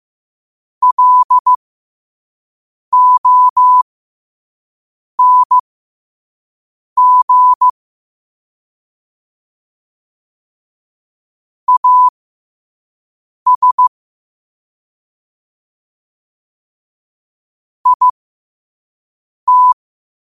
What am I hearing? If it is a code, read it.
Morse code, heard as a keyed tone, about 15 wpm, spaced as for 5 wpm, 1000 Hz. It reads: LONG AS IT
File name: qotdfiles / 5wpm